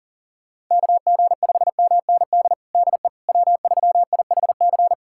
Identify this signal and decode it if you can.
Morse code, heard as a keyed tone, about 40 wpm, 700 Hz. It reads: KG5MND DE W3IHC
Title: USA Callsign DE Callsign 40wpm